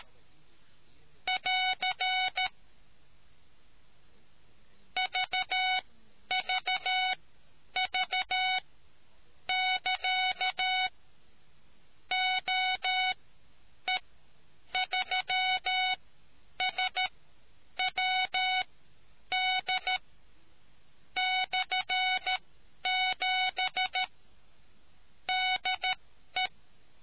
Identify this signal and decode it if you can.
Morse code, heard as a keyed tone, about 13 wpm, spaced as for 8 wpm, 740 Hz. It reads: <AR> VVV <KA> OE3SWD/7 DE